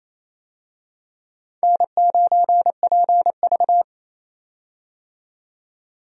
Morse code, heard as a keyed tone, about 28 wpm, 700 Hz. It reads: N9PV
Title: USA Call Signs - 4 Characters - 28wpm